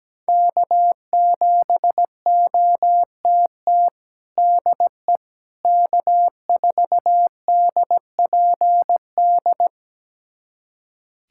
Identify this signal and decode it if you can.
Morse code, heard as a keyed tone, about 17 wpm, 700 Hz. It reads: K7OTT DE K4DPD
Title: USA Callsign DE Callsign - 5 Characters - 17wpm